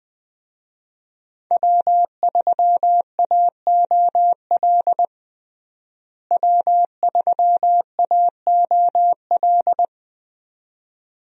Morse code, heard as a keyed tone, about 21 wpm, 700 Hz. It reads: W3AOL W3AOL
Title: USA Call Signs - 5 Characters - Repeated 2x - 20wpm